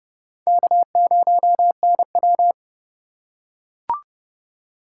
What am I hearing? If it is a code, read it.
Morse code, heard as a keyed tone, about 30 wpm, 700 Hz. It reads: K0NW
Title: K1USN SST QSO Practice - 30wpm